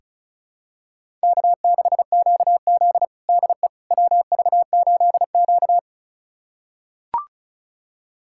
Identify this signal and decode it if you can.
Morse code, heard as a keyed tone, about 36 wpm, 700 Hz. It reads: K6QZ DE WV8Q
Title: USA Callsign DE Callsign - 4 Characters - 35wpm